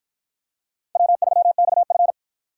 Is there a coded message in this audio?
Yes: Morse code from 1.0 to 2.1 s, heard as a keyed tone, about 51 wpm, 700 Hz. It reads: W3XF